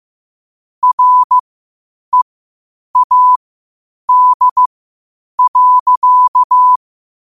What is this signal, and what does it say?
Morse code, heard as a keyed tone, about 15 wpm, spaced as for 8 wpm, 1000 Hz. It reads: READ.
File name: 8wpm